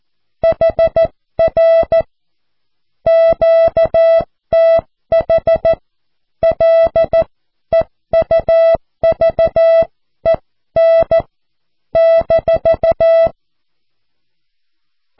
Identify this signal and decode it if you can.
Morse code, heard as a keyed tone, about 14 wpm, spaced as for 12 wpm, 660 Hz. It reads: HR QTH LEUVEN -
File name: cw / QSO